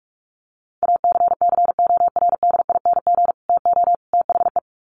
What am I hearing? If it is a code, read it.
Morse code, heard as a keyed tone, about 44 wpm, 700 Hz. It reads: ACCORDING TO THE
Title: Sets of 3 Words 45wpm